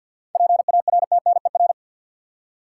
Morse code, heard as a keyed tone, about 52 wpm, 700 Hz. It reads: PARTNER
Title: Seven Letter Words - 50wpm